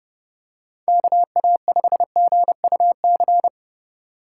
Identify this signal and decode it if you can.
Morse code, heard as a keyed tone, about 28 wpm, 700 Hz. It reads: KA5GUC